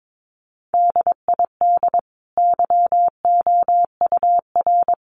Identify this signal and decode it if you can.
Morse code, heard as a keyed tone, about 22 wpm, 700 Hz. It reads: DID YOUR
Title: Sentences from Top 1100 Words 22wpm